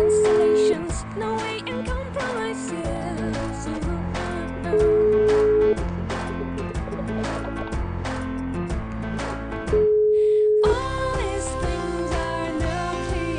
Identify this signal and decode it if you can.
Ringback tone (Europe, 425 Hz), with noise.